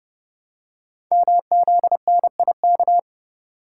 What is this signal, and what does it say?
Morse code, heard as a keyed tone, about 30 wpm, 690 Hz. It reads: MZNIK
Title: Code Groups - 5 Letters - 30wpm